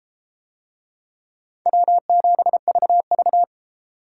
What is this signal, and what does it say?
Morse code, heard as a keyed tone, about 32 wpm, 700 Hz. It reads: W7VV